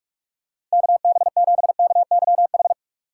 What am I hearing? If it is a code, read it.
Morse code, heard as a keyed tone, about 47 wpm, 700 Hz. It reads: KB7KYH
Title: USA Call Signs - 6 Characters - 6x Character Spacing - 45wpm